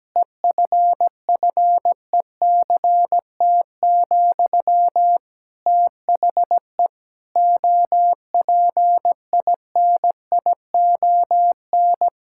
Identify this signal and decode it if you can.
Morse code, heard as a keyed tone, about 17 wpm, 700 Hz. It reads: EFFECT, THE OPINION
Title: Sentences from Top 1600 Words 17wpm